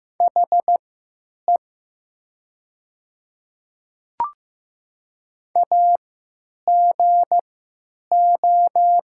Morse code, heard as a keyed tone, about 15 wpm, spaced as for 9 wpm, 700 Hz. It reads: HE AGO
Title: 3x Character Spacing - Top 500 Words - 15wpm